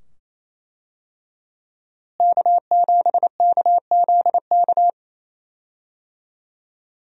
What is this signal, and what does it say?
Morse code, heard as a keyed tone, about 27 wpm, 700 Hz. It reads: K7KZK
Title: USA Call Signs - 5 Characters - Repeated 2x - 28wpm